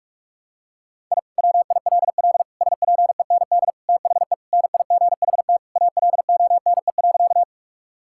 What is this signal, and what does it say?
Morse code, heard as a keyed tone, about 44 wpm, 700 Hz. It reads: I WILL SPEND THE NIGHT ALONE.
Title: Sentences from Top 1000 Words - Review Part 5 - 45wpm